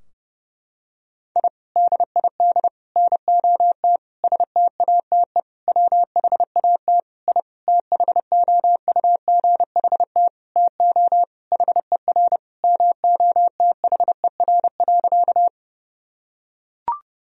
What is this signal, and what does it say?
Morse code, heard as a keyed tone, about 28 wpm, 700 Hz. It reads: I DID NOT STATE WHAT I THOUGHT TO HER MOTHER.
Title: Sentences from Top 1000 Words - Review Part 2 - 30wpm